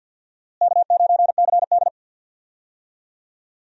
Morse code, heard as a keyed tone, about 52 wpm, 700 Hz. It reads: K9CD
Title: USA Call Signs - 4 Characters - Repeated 2x - 50wpm